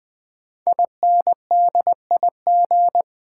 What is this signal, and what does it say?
Morse code, heard as a keyed tone, about 20 wpm, 700 Hz. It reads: INDIG
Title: Build A Word - 5 Letters At A Time - 10-Letter Words - 20wpm